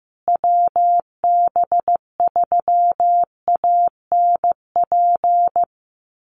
Morse code, heard as a keyed tone, about 15 wpm, 700 Hz. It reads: WB3ANP
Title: USA Call Signs Repeated 2x 15wpm